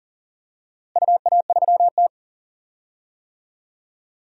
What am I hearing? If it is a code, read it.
Morse code, heard as a keyed tone, about 40 wpm, 700 Hz. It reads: UA3T
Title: DXCC-100 Call Signs - 4 Characters - Repeated 3x - 40wpm